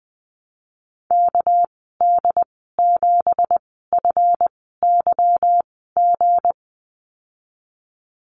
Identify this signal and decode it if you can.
Morse code, heard as a keyed tone, about 20 wpm, spaced as for 15 wpm, 700 Hz. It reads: KD7FYG
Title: 245 Lesson 39g - Letter z - Call Signs 20wpm